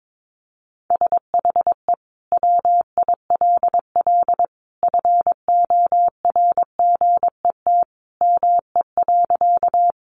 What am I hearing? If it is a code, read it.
Morse code, heard as a keyed tone, about 22 wpm, 700 Hz. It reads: SHE WILL FORGET ME.
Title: Sentences from Top 1200 Words 22wpm